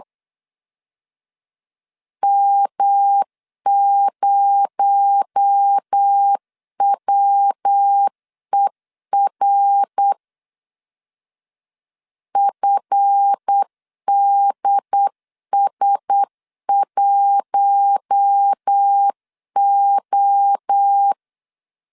Morse code, heard as a keyed tone, about 8 wpm, 800 Hz. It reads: M0WER FDS1O
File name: LEZIONI / 7 GF9Z / 7 GF 8wpm 3108